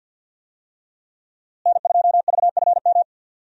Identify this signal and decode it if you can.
Morse code, heard as a keyed tone, about 53 wpm, 700 Hz. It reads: N2VFM
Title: USA Call Signs - 5 Characters - Repeated 3x - 50wpm